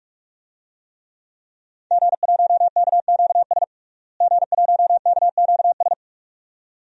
Morse code, heard as a keyed tone, about 46 wpm, 700 Hz. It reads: G1KQS G1KQS
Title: DXCC-100 Call Signs - 5 Characters - Repeated 2x - 45wpm